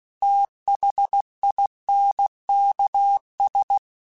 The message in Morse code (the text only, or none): THINKS